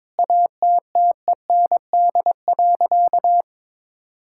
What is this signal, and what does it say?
Morse code, heard as a keyed tone, about 22 wpm, 700 Hz. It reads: ATTEND.
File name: Sentences with 7-Word Phrases - 22wpm